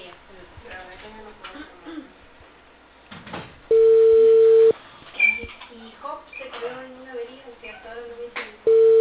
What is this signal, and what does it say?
Ringback tone (China, 450 Hz), with speech.